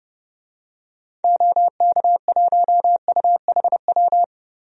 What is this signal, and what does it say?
Morse code, heard as a keyed tone, about 30 wpm, 700 Hz. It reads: OK1UHW
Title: DXCC-100 Call Signs - 6 Characters - Repeated 3x - 30wpm